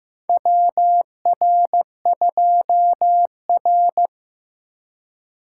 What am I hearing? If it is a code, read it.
Morse code, heard as a keyed tone, about 15 wpm, 700 Hz. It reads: WR2R
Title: USA Call Signs - 4 Characters - 2x Character Spacing - 15wpm